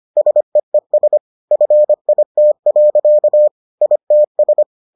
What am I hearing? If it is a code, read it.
Morse code, heard as a keyed tone, about 25 wpm, 600 Hz. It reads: SEES FIT. ITS